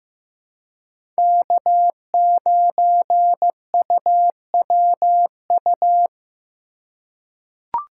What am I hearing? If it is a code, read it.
Morse code, heard as a keyed tone, about 15 wpm, 700 Hz. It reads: K9UWU
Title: USA Call Signs - 5 Characters - 3x Character Spacing - 15wpm